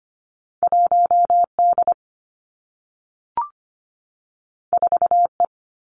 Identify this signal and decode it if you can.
Morse code, heard as a keyed tone, about 25 wpm, 700 Hz. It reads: 1D 4E
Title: Two Challenging Letter-Number Combinations 25wpm